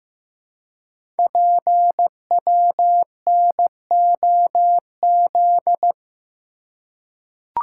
Morse code, heard as a keyed tone, about 15 wpm, 700 Hz. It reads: PWNOZ
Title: Code Groups - 5 Letters - 15wpm